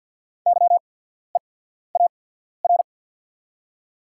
Morse code, heard as a keyed tone, about 53 wpm, spaced as for 14 wpm, 700 Hz. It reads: YEAR
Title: 8x Character Spacing - Top 300 Words - 50wpm